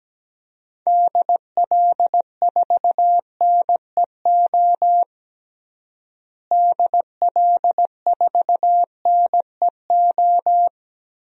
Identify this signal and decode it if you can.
Morse code, heard as a keyed tone, about 17 wpm, 700 Hz. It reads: DL4NEO DL4NEO
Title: DXCC-100 Call Signs - 6 Characters - Repeated 2x - 17wpm